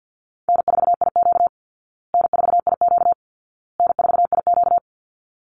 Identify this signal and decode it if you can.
Morse code, heard as a keyed tone, about 50 wpm, 700 Hz. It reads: N4IQ N4IQ N4IQ
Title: Easy Call Signs Repeated 3x - 50wpm